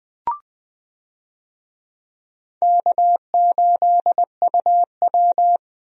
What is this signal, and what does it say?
Morse code, heard as a keyed tone, about 20 wpm, 700 Hz. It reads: K8UW
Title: USA Call Signs - 4 Characters - 20wpm